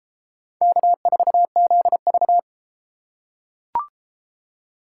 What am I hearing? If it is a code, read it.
Morse code, heard as a keyed tone, about 32 wpm, 700 Hz. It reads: K4ZV